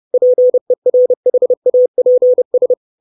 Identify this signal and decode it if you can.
Morse code, heard as a keyed tone, about 30 wpm, 500 Hz. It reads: PERHAPS
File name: 632 Words 30WPM